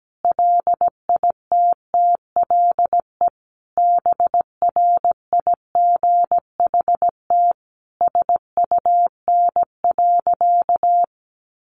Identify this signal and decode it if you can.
Morse code, heard as a keyed tone, about 17 wpm, 700 Hz. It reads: LITTLE BRIGHT SUN.
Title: Sentences from Top 1000 Words 17wpm